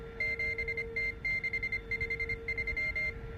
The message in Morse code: 7T653